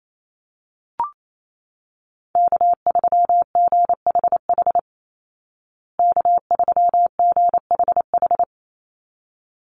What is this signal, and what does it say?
Morse code, heard as a keyed tone, about 28 wpm, 700 Hz. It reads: K3GHH K3GHH
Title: Call Signs Repeated 3x - 28wpm